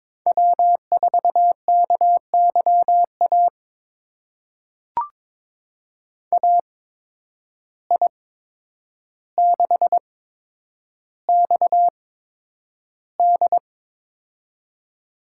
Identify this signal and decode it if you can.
Morse code, heard as a keyed tone, about 23 wpm, spaced as for 10 wpm, 700 Hz. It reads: W4KYA AI6XD
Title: USA Call Signs - 5 Characters - 8x Character Spacing - 22wpm